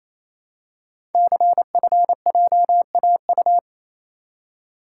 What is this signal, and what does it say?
Morse code, heard as a keyed tone, about 29 wpm, 700 Hz. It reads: CFJAU
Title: Code Groups - 5 Letters - 28wpm